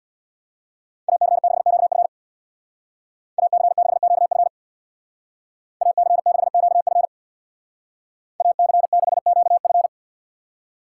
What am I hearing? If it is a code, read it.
Morse code, heard as a keyed tone, about 65 wpm, 700 Hz. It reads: AC6QF AC6QF AC6QF AC6QF